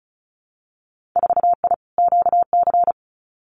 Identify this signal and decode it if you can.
Morse code, heard as a keyed tone, about 36 wpm, 700 Hz. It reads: 4I QC
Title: Winter Field Day Exchanges Encore 35wpm